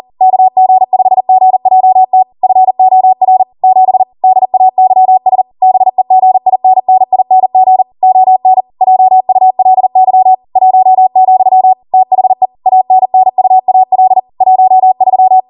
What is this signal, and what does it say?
Morse code, heard as a keyed tone, about 40 wpm, 750 Hz. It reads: KG5GJT FOR 7 DAYS BEGINNING ON JULY 1, THE ANNUAL 13